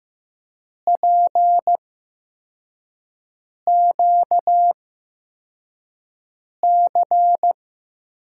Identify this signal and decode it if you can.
Morse code, heard as a keyed tone, about 15 wpm, spaced as for 4 wpm, 700 Hz. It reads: PQC
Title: USA Call Signs - 6 Characters - 8x Character Spacing - 15wpm